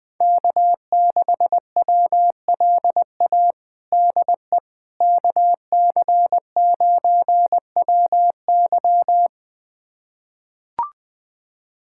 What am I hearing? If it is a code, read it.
Morse code, heard as a keyed tone, about 20 wpm, 700 Hz. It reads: K6WLA DE KC9WY
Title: USA Callsign DE Callsign 20wpm